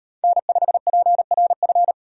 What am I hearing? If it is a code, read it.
Morse code, heard as a keyed tone, about 35 wpm, 700 Hz. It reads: N5PRF